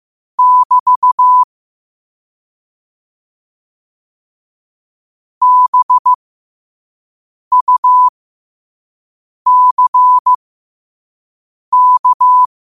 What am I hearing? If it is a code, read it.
Morse code, heard as a keyed tone, about 15 wpm, spaced as for 5 wpm, 1000 Hz. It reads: <BT> BUCK
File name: qotdfiles / 5wpm